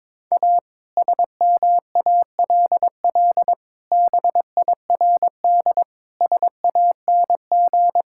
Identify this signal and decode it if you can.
Morse code, heard as a keyed tone, about 22 wpm, 700 Hz. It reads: A SMALL BIRD SANG
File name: practice / 22wpm / Sentences with 2-Word Phrases - 22wpm